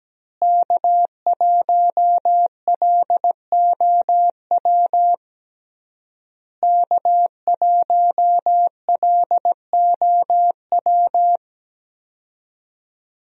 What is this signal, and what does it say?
Morse code, heard as a keyed tone, about 17 wpm, 700 Hz. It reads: K1LOW K1LOW
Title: USA Call Signs - 5 Characters - Repeated 3x - 17wpm